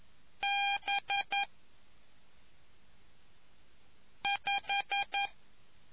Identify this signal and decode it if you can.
Morse code, heard as a keyed tone, about 11 wpm, 800 Hz. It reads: B 5